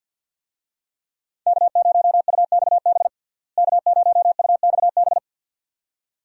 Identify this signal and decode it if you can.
Morse code, heard as a keyed tone, about 52 wpm, 700 Hz. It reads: K0UXB K0UXB
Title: USA Call Signs - 5 Characters - Repeated 2x - 50wpm